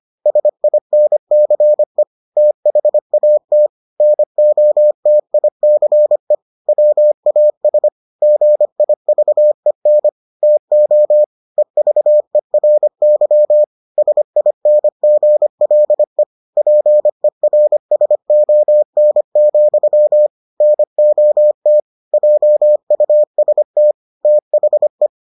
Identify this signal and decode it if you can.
Morse code, heard as a keyed tone, about 25 wpm, 600 Hz. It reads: SINCE THAT NOTICE WAS GIVEN TO EVERY SINGLE PERSON, NOT JUST THE